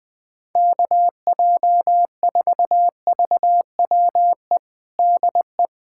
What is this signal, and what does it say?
Morse code, heard as a keyed tone, about 21 wpm, 700 Hz. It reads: KJ4VWE DE
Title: USA Callsign DE Callsign - 6 Characters - 20wpm